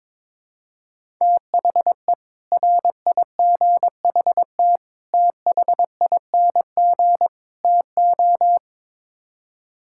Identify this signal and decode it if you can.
Morse code, heard as a keyed tone, about 21 wpm, 700 Hz. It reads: THE RIGHT THING TO
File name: practice / 22wpm / Build A Phrase - 5-Word Phrases - 22wpm